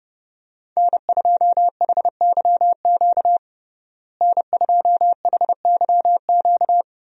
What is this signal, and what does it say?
Morse code, heard as a keyed tone, about 30 wpm, 700 Hz. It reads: N2HYQ N2HYQ
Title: USA Call Signs - 5 Characters - Repeated 2x - 30wpm